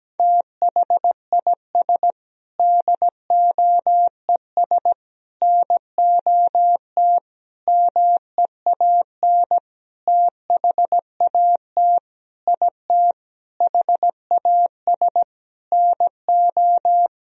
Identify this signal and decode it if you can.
Morse code, heard as a keyed tone, about 17 wpm, 700 Hz. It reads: THIS DOES NOT MEAN THAT IT HAS NO
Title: Sentences from Top 1000 Words - Review Part 4 - 17wpm